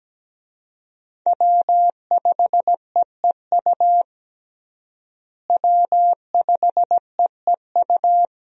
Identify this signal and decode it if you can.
Morse code, heard as a keyed tone, about 17 wpm, 700 Hz. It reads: W5EEU W5EEU
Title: USA Call Signs - 5 Characters - Repeated 2x - 17wpm